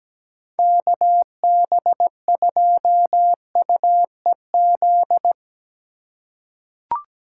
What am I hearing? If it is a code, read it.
Morse code, heard as a keyed tone, about 17 wpm, 700 Hz. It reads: KB2UEZ
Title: USA Call Signs - 6 Characters - 4x Character Spacing - 17wpm